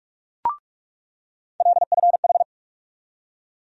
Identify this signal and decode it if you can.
Morse code, heard as a keyed tone, about 46 wpm, 700 Hz. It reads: LFH
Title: USA Call Sign Postfixes - 3 Letters 45wpm